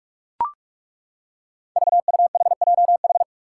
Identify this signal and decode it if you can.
Morse code, heard as a keyed tone, about 42 wpm, 700 Hz. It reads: VUHJH
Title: Five Challenging Letter-Number Combinations 45wpm